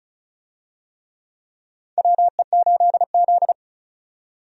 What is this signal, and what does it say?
Morse code, heard as a keyed tone, about 34 wpm, 700 Hz. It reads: WE8Z
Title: USA Call Signs - 4 Characters - 35wpm